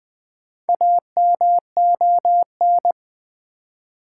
Morse code, heard as a keyed tone, about 20 wpm, 700 Hz. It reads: AMON
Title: Build A Word - 1 Letter At A Time - 5-Letter Words - 20wpm